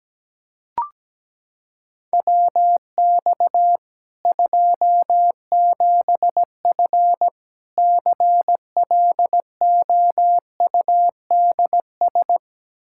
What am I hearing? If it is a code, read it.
Morse code, heard as a keyed tone, about 17 wpm, 700 Hz. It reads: WX 27F CLOUDS
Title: Practice QSO - Exchange 2 - Info - 17wpm